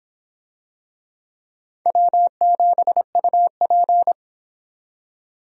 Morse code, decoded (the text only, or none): W7UP